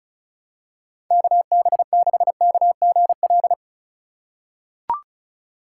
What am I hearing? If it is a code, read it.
Morse code, heard as a keyed tone, about 36 wpm, 700 Hz. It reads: KB6KGL
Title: USA Call Signs - 3x Character Spacing - 35wpm